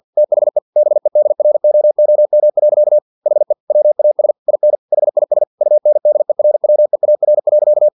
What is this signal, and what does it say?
Morse code, heard as a keyed tone, about 50 wpm, 600 Hz. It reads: THE BEDROOM. HE WAS IN HIS UNDERWEAR.
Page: Never 49wpm